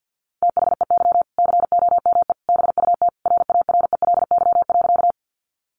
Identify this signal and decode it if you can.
Morse code, heard as a keyed tone, about 50 wpm, 700 Hz. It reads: THEY COME BUT RARELY.
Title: Sentences from Top 1400 Words 50wpm